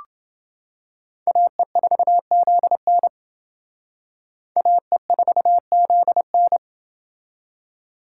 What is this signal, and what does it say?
Morse code, heard as a keyed tone, about 29 wpm, 700 Hz. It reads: AE4ZN AE4ZN